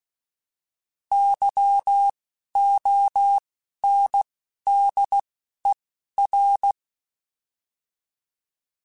Morse code, heard as a keyed tone, about 16 wpm, 800 Hz. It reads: Y O N D E R
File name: Alphabetical / Alphabetical mp3s / Ys @10